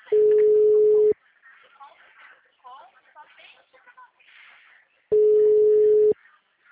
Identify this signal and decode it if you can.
Ringback tone (Europe, 425 Hz), with speech.